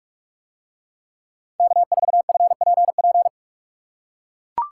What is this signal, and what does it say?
Morse code, heard as a keyed tone, about 43 wpm, 700 Hz. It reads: K4FPP